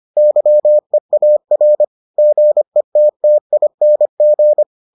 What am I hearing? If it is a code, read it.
Morse code, heard as a keyed tone, about 25 wpm, 600 Hz. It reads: YEAR GETTING